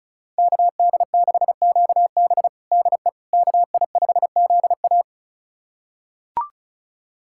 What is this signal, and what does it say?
Morse code, heard as a keyed tone, about 36 wpm, 700 Hz. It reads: KD6QB DE KI5ZA
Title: USA Callsign DE Callsign - 5 Characters - 35wpm